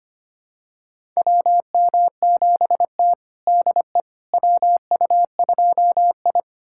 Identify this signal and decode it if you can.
Morse code, heard as a keyed tone, about 25 wpm, 700 Hz. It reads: WM7T DE WU2I
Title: USA Callsign DE Callsign - 4 Characters - 25wpm